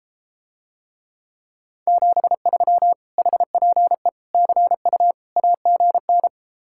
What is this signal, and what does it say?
Morse code, heard as a keyed tone, about 33 wpm, 700 Hz. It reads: 73 HPE CU AGN